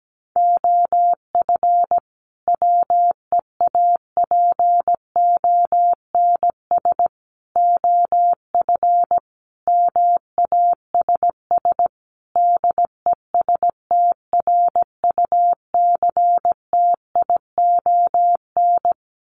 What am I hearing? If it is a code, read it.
Morse code, heard as a keyed tone, about 17 wpm, 700 Hz. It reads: OF WEAPONS OF MASS DESTRUCTION
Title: Sets of 5 Words Encore 17wpm